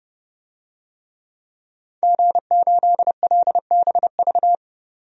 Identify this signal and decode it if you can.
Morse code, heard as a keyed tone, about 30 wpm, 700 Hz. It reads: G8LBV
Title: DXCC-100 Call Signs - 5 Characters - 30wpm